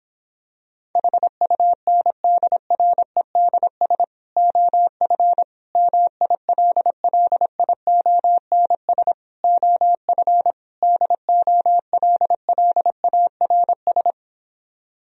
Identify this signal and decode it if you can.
Morse code, heard as a keyed tone, about 26 wpm, 700 Hz. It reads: HUNDREDS OF MILLIONS OF DOLLARS